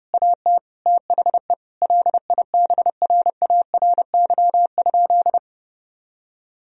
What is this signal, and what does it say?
Morse code, heard as a keyed tone, about 30 wpm, 700 Hz. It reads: AT THE LIBRARY?
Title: Sentences from Top 1000 Words - Review Part 4 - 30wpm